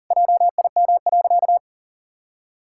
Morse code, heard as a keyed tone, about 40 wpm, 700 Hz. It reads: JIM.
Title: Pangram Challenge v2 - 40wpm